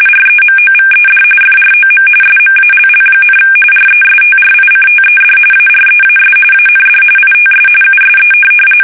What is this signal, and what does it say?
RTTY, about 45.45 baud. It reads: REMOVED FROM SMALL, RESTRICTED AREAS AND ITS USE
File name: fsk850 / benjamin-g-lamme